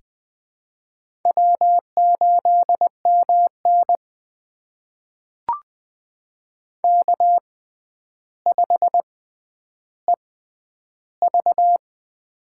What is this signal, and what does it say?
Morse code, heard as a keyed tone, about 20 wpm, spaced as for 10 wpm, 700 Hz. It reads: W8MN K5EV
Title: USA Call Signs - 4 Characters - 6x Character Spacing - 20wpm